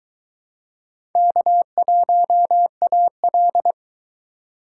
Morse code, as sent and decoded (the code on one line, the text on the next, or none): -.- .---- .- .-..
K1AL